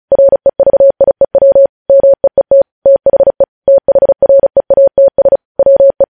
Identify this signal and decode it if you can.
Morse code, heard as a keyed tone, about 36 wpm, 550 Hz. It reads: REVIEW MEET THE THREATS WE
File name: podcast / News headlines-1x-35x35